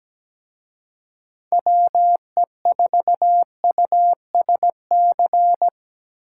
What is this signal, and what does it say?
Morse code, heard as a keyed tone, about 20 wpm, 700 Hz. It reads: WE4USC
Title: Speed Racing - USA 2x3 Call Signs 17wpm